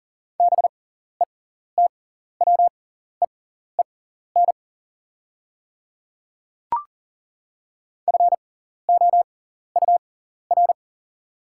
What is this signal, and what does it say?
Morse code, heard as a keyed tone, about 38 wpm, spaced as for 14 wpm, 690 Hz. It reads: BETWEEN FOUR
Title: 6x Character Spacing - Top 300 Words - 40wpm